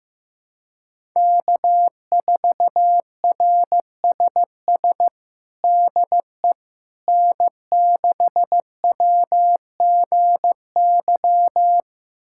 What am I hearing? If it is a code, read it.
Morse code, heard as a keyed tone, about 15 wpm, 700 Hz. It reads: K4RSS DE N6WGY
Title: USA Callsign DE Callsign 15wpm